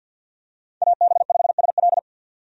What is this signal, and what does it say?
Morse code, heard as a keyed tone, about 46 wpm, 700 Hz. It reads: AB5SL
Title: USA Call Signs - 5 Characters - 50wpm